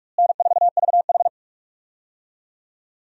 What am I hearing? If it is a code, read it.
Morse code, heard as a keyed tone, about 44 wpm, 700 Hz. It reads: N4VH
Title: USA Call Signs - 4 Characters - Repeated 3x - 45wpm